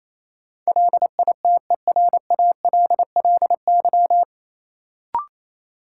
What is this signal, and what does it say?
Morse code, heard as a keyed tone, about 28 wpm, 700 Hz. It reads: LITERALLY
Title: Sentences from Top 1400 Words - New Word Drill Alt - 28wpm